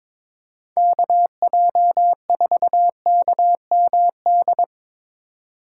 Morse code, heard as a keyed tone, about 22 wpm, 700 Hz. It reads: KJ4KMD
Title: USA Call Signs - 6 Characters - 6x Character Spacing - 22wpm